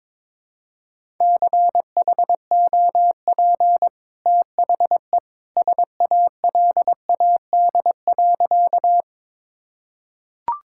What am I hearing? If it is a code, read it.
Morse code, heard as a keyed tone, about 21 wpm, 700 Hz. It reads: CHOP THE SALAD.
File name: practice / 22wpm / Easy Sentences with Hints - 22wpm